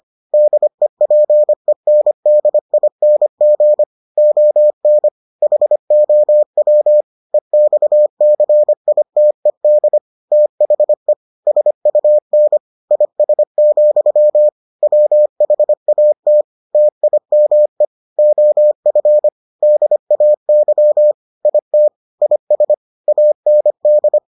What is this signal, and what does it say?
Morse code, heard as a keyed tone, about 25 wpm, 600 Hz. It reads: DEPENDING ON HOW EXCITED THE SUN IS, WHAT TIME OF DAY IT IS AND